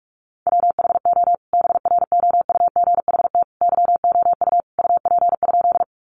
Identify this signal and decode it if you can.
Morse code, heard as a keyed tone, about 44 wpm, 700 Hz. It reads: WHO BROUGHT YOU UP?
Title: Sentences from Top 500 Words 45wpm